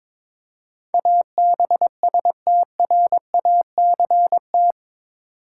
Morse code, heard as a keyed tone, about 21 wpm, 700 Hz. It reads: ABSTRACT
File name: practice / 22wpm / Eight Letter Words - 22wpm